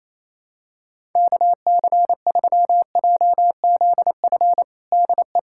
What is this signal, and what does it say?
Morse code, heard as a keyed tone, about 28 wpm, 700 Hz. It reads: KC3JZF DE
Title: USA Callsign DE Callsign 28wpm